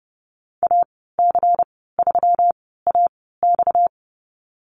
Morse code, heard as a keyed tone, about 30 wpm, spaced as for 17 wpm, 700 Hz. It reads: AC3AX
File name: Lesson 37g - Letter x - Call Signs 30wpm